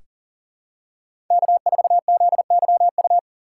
Morse code, heard as a keyed tone, about 40 wpm, 700 Hz. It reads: K4ZYU
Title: USA Call Signs - 5 Characters - 2x Character Spacing - 40wpm